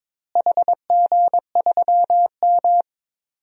Morse code, heard as a keyed tone, about 22 wpm, 700 Hz. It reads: HG3M
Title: DXCC-100 Call Signs - 4 Characters - Repeated 3x - 22wpm